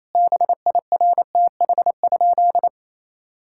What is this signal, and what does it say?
Morse code, heard as a keyed tone, about 28 wpm, 700 Hz. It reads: BIRTH?
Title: Sentences from Top 1000 Words - Review Part 5 - 28wpm